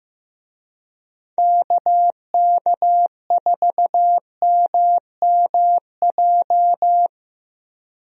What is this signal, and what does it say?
Morse code, heard as a keyed tone, about 15 wpm, 700 Hz. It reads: KK4MMJ
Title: USA Call Signs - 6 Characters - 15wpm